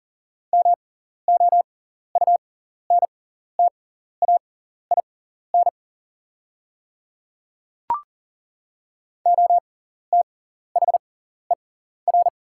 Morse code, heard as a keyed tone, about 40 wpm, spaced as for 14 wpm, 700 Hz. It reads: MOUNTAIN OTHER
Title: 6x Character Spacing - Top 300 Words - 40wpm